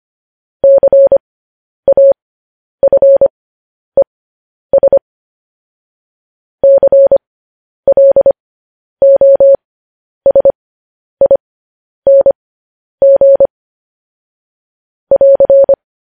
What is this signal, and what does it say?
Morse code, heard as a keyed tone, about 25 wpm, spaced as for 10 wpm, 550 Hz. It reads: CAFES CLOSING <AR>
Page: news-headlines-in-morse-code-at-10-wpm